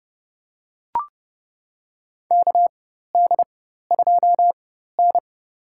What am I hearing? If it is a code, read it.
Morse code, heard as a keyed tone, about 30 wpm, spaced as for 14 wpm, 700 Hz. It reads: KD2N
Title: USA Call Signs - 4x Character Spacing - 30wpm